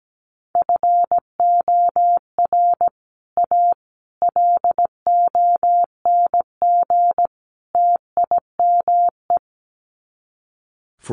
Morse code, heard as a keyed tone, about 17 wpm, 700 Hz. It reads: FOR A LONG TIME
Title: Sets of 4 Words 17wpm